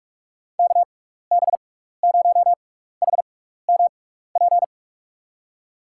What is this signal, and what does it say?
Morse code, heard as a keyed tone, about 47 wpm, spaced as for 16 wpm, 700 Hz. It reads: KB0HMP
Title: USA Call Signs - 6x Character Spacing - 45wpm